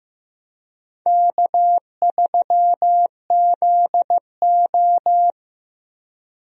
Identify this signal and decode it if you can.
Morse code, heard as a keyed tone, about 15 wpm, 700 Hz. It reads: K3ZO
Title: USA Call Signs - 4 Characters - 15wpm